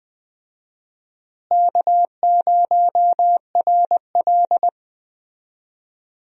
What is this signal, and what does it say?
Morse code, heard as a keyed tone, about 20 wpm, 700 Hz. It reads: K0RL
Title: USA Call Signs - 4 Characters - Repeated 2x - 20wpm